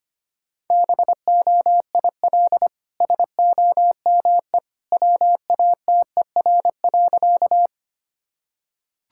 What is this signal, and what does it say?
Morse code, heard as a keyed tone, about 25 wpm, 700 Hz. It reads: BOIL SOME WATER.
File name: Easy Sentences with Hints - 25wpm